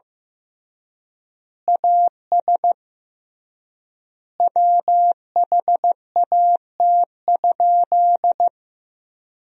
Morse code, heard as a keyed tone, about 15 wpm, 700 Hz. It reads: AS WHAT?
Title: 3x Word-Spacing - Sentences from Top 200 Words - 15wpm